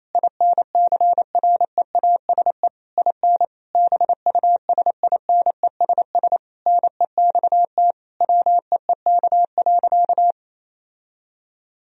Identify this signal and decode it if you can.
Morse code, heard as a keyed tone, about 28 wpm, 700 Hz. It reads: INCREASE IN BUSINESS NEXT WEEK.
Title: Sentences from Top 1000 Words - Review Part 4 - 28wpm